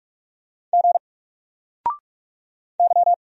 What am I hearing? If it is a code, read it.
Morse code, heard as a keyed tone, about 44 wpm, 700 Hz. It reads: G Y
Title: Single Character (Letter:Number:Punctuation:Prosign) Alt 45wpm